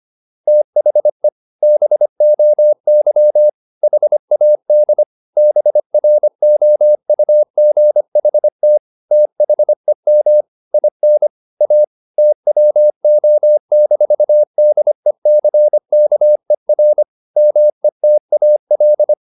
Morse code, heard as a keyed tone, about 25 wpm, 600 Hz. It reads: THE BOY HAD BROUGHT THEM IN A TWO-DECKER METAL
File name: the old man and the sea part 1 25wpm